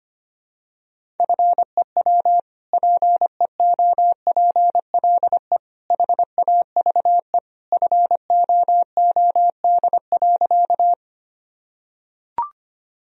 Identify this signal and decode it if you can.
Morse code, heard as a keyed tone, about 25 wpm, 700 Hz. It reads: FEW PEOPLE HAVE FOOD.
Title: Sentences from Top 1000 Words - Review Part 1 - 25wpm